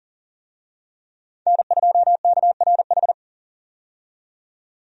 Morse code, heard as a keyed tone, about 40 wpm, 700 Hz. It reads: N2KRH
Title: USA Call Signs - 5 Characters - 40wpm